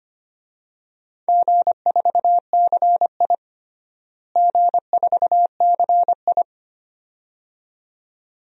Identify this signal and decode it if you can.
Morse code, heard as a keyed tone, about 25 wpm, 700 Hz. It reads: G4CI G4CI
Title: DXCC-100 Call Signs - 4 Characters - Repeated 2x - 25wpm